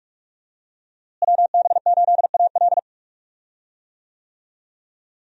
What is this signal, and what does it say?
Morse code, heard as a keyed tone, about 46 wpm, 700 Hz. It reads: WB8AL
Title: USA Call Signs - 5 Characters - Repeated 2x - 45wpm